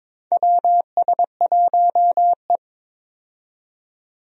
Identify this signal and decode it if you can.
Morse code, heard as a keyed tone, about 22 wpm, 700 Hz. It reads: WS1E
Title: USA Call Signs - 4 Characters - 22wpm